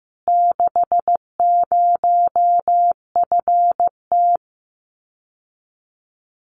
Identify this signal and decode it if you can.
Morse code, heard as a keyed tone, about 15 wpm, 700 Hz. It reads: 60FT
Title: Common QSO Phrases Encore 15wpm